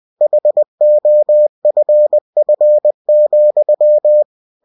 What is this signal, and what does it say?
Morse code, heard as a keyed tone, about 20 wpm, 600 Hz. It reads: HOFF,